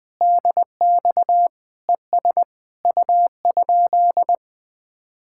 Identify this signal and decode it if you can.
Morse code, heard as a keyed tone, about 20 wpm, 700 Hz. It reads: DX ES U?
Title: Speed-Racing Common QSO Phrases - 20wpm